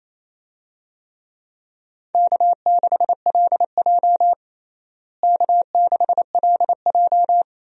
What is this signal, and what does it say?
Morse code, heard as a keyed tone, about 28 wpm, 700 Hz. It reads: K6LJ K6LJ
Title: USA Call Signs - 4 Characters - Repeated 3x - 28wpm